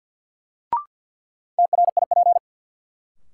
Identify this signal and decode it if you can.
Morse code, heard as a keyed tone, about 51 wpm, 700 Hz. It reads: TRIP
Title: Sentences from Top 1000 Words - Review Part 4 - New Word Drill - 50wpm